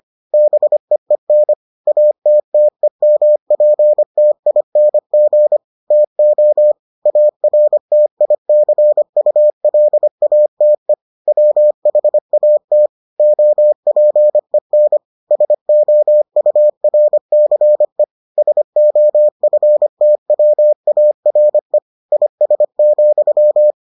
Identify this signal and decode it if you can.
Morse code, heard as a keyed tone, about 25 wpm, 600 Hz. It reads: BEEN ATTEMPTING TO ARTICULATE WHAT OPEN SOURCE SOFTWARE IS,